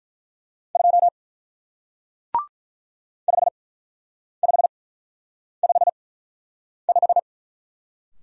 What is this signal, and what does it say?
Morse code, heard as a keyed tone, about 46 wpm, spaced as for 9 wpm, 700 Hz. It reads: 2 5555